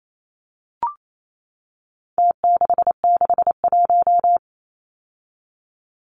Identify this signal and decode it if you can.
Morse code, heard as a keyed tone, about 28 wpm, 700 Hz. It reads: T661
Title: Numbers - Contest Style - 4-Digit 28wpm